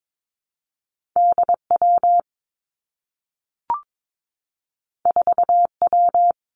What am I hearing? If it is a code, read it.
Morse code, heard as a keyed tone, about 22 wpm, 700 Hz. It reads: D W 4 W
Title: Two Challenging Letter-Number Combinations 22wpm